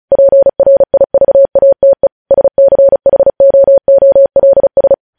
Morse code, heard as a keyed tone, about 36 wpm, 550 Hz. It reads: PRIVATE SCHOOLS
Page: news headlines in morse code at 35 wpm